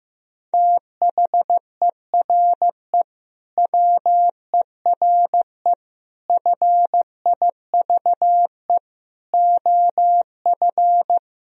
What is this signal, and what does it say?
Morse code, heard as a keyed tone, about 16 wpm, 700 Hz. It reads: THERE WERE FIVE OF
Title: Sentences from Top 1000 Words - Review Part 2 - 15wpm